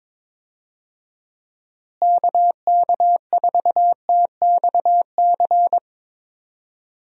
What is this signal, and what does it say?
Morse code, heard as a keyed tone, about 22 wpm, 700 Hz. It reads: KK4TXC
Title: USA Call Signs - 6 Characters - Repeated 3x - 22wpm